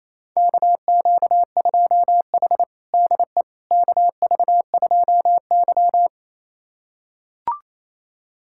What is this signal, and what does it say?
Morse code, heard as a keyed tone, about 28 wpm, 700 Hz. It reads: KQ2H DE KV2Y
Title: USA Callsign DE Callsign - 4 Characters - 28wpm